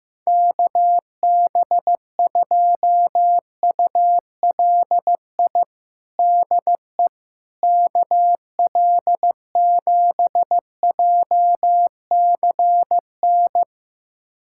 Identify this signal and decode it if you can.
Morse code, heard as a keyed tone, about 15 wpm, 700 Hz. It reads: KB2ULI DE KL7JCN
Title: USA Callsign DE Callsign 15wpm